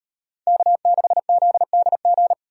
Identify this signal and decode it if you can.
Morse code, heard as a keyed tone, about 38 wpm, 700 Hz. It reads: K6ZDG